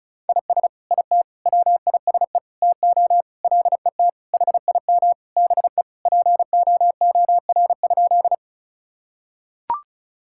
Morse code, heard as a keyed tone, about 36 wpm, 700 Hz. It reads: IS IT WISE TO LET HIM BE POOR?
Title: Sentences from Top 1600 Words 35wpm